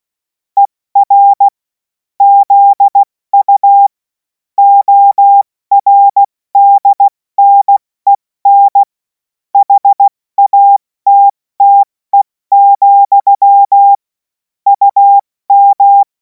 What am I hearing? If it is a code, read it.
Morse code, heard as a keyed tone, about 16 wpm, 800 Hz. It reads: ER ZU ORDNEN HATTE, UM
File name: Morsekurs / Witze / Witz 24. Februar 2026 70 F80